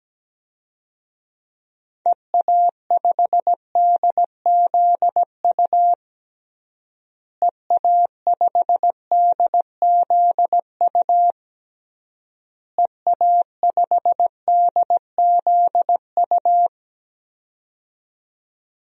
Morse code, heard as a keyed tone, about 17 wpm, 700 Hz. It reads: EA5DZU EA5DZU EA5DZU
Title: DXCC-100 Call Signs - 6 Characters - Repeated 3x - 17wpm